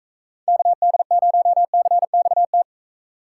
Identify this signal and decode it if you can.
Morse code, heard as a keyed tone, about 42 wpm, 700 Hz. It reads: KD0CXT